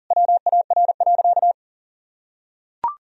Morse code, heard as a keyed tone, about 40 wpm, 700 Hz. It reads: WAR.
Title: Sentences from Top 1000 Words - Review Part 3 - 40wpm